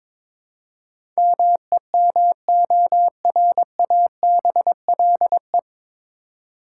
Morse code, heard as a keyed tone, about 22 wpm, 700 Hz. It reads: MEMORABLE
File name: 9 Letter Words - First 3 Letters Given - 22wpm